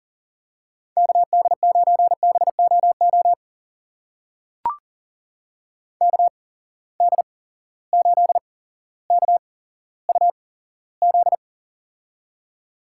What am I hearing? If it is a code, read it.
Morse code, heard as a keyed tone, about 40 wpm, spaced as for 17 wpm, 700 Hz. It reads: KD9BOO KD8KUZ
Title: USA Call Signs - 6 Characters - 8x Character Spacing - 40wpm